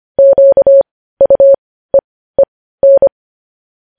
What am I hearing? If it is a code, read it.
Morse code, heard as a keyed tone, about 25 wpm, spaced as for 15 wpm, 550 Hz. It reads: QUEEN
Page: News Headlines in Morse Code at 15 WPM